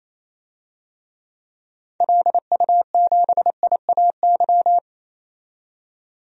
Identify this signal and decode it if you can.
Morse code, heard as a keyed tone, about 27 wpm, 700 Hz. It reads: LU7IAY
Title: DXCC-100 Call Signs - 6 Characters - 28wpm